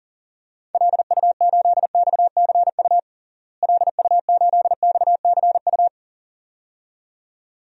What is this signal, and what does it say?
Morse code, heard as a keyed tone, about 40 wpm, 700 Hz. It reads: LU8XCU LU8XCU
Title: DXCC-100 Call Signs - 6 Characters - Repeated 2x - 40wpm